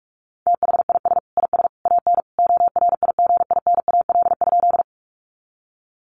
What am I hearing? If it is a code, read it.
Morse code, heard as a keyed tone, about 44 wpm, 700 Hz. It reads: THIS IS AN ORIGINAL?
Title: Sentences from Top 900 Words 45wpm